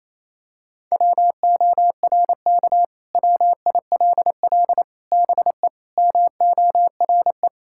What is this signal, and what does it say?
Morse code, heard as a keyed tone, about 28 wpm, 700 Hz. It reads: WORK WILL BE MORE
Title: Sentences from Top 1000 Words - Review Part 6 - 28wpm